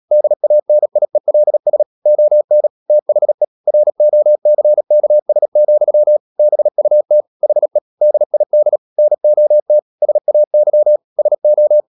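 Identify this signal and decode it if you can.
Morse code, heard as a keyed tone, about 37 wpm, 600 Hz. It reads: DANIELS ON THE ROCKS, BUT HE DID NOT SAY SO
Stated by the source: Morse code practice at 35–39 wpm sending English text of Ken Follett's novel Never, from CW med Gnister.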